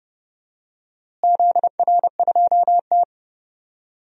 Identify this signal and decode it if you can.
Morse code, heard as a keyed tone, about 31 wpm, 700 Hz. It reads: ZR2T
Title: DXCC-100 Call Signs - 4 Characters - Repeated 3x - 30wpm